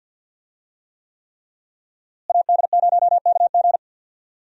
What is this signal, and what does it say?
Morse code, heard as a keyed tone, about 52 wpm, 700 Hz. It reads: AD0KG
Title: USA Call Signs - 5 Characters - Repeated 3x - 50wpm